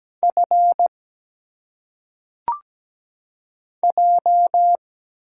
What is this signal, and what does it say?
Morse code, heard as a keyed tone, about 17 wpm, 700 Hz. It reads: F J
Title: Single Letters - Phonetic Pronunciation 17wpm